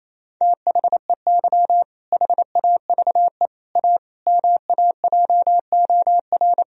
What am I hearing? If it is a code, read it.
Morse code, heard as a keyed tone, about 28 wpm, 700 Hz. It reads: THEY HAVE A MAJOR
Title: Sentences from Top 1000 Words - Review Part 4 - 28wpm